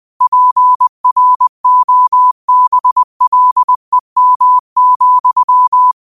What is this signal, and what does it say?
Morse code, heard as a keyed tone, about 20 wpm, 1000 Hz. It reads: PROBLEM,
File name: qotdfiles / 20wpm